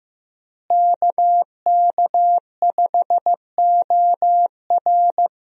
Morse code, heard as a keyed tone, about 15 wpm, 700 Hz. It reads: KK5OR
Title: USA Call Signs - 5 Characters - Repeated 3x - 15wpm